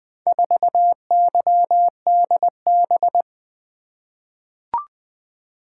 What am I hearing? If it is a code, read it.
Morse code, heard as a keyed tone, about 20 wpm, 700 Hz. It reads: 4YDB
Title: Four Letter-Number Combinations - 20wpm